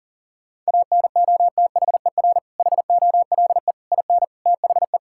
Morse code, heard as a keyed tone, about 40 wpm, 700 Hz. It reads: ANOTHER HOLE IN THE
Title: Sentences from Top 1600 Words 40wpm